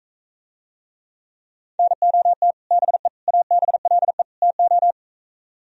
Morse code, heard as a keyed tone, about 42 wpm, 700 Hz. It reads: NOT BE ABLE TO